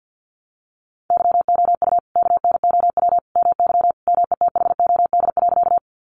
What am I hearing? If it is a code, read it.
Morse code, heard as a keyed tone, about 50 wpm, 700 Hz. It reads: YOU KNOW MY METHOD.
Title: Sentences from Top 600 Words 50wpm